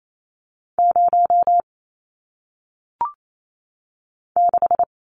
Morse code, heard as a keyed tone, about 28 wpm, 700 Hz. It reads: 0 6
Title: Numbers - 1-Digit 28wpm